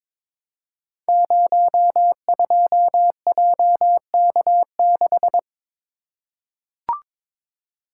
Morse code, heard as a keyed tone, about 22 wpm, 700 Hz. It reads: 02JK6
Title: Code Groups - 5 Letter:Numbers - 22wpm